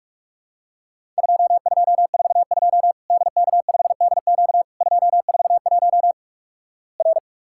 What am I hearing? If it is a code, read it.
Morse code, heard as a keyed tone, about 46 wpm, 700 Hz. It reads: 2242 DK5DQ 241 R
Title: Worked All Europe DX Contest - QTC Exchange - 45wpm